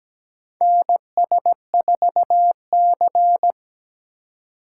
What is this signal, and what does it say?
Morse code, heard as a keyed tone, about 17 wpm, 700 Hz. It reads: NS4C
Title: USA Call Signs - 3x Character Spacing - 17wpm